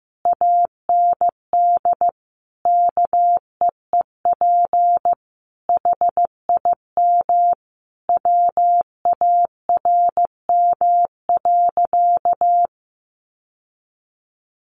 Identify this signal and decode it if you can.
Morse code, heard as a keyed tone, about 15 wpm, 700 Hz. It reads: AND KEEP HIM WARM.
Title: Sentences from Top 1500 Words 15wpm